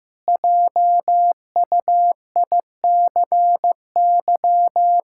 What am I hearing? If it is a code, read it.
Morse code, heard as a keyed tone, about 15 wpm, 700 Hz. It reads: JUICY
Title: Pangram Challenge v2 - 15wpm